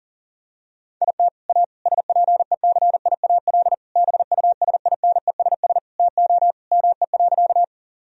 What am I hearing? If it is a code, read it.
Morse code, heard as a keyed tone, about 40 wpm, 700 Hz. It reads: IT A SPECIAL BUSINESS TO ME.
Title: Sentences from Top 1000 Words - Review Part 3 - 40wpm